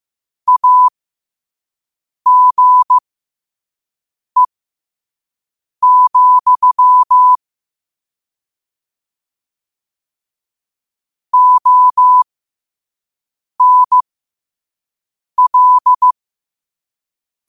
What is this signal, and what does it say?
Morse code, heard as a keyed tone, about 15 wpm, spaced as for 5 wpm, 1000 Hz. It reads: AGE, ONL
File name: qotdfiles / 5wpm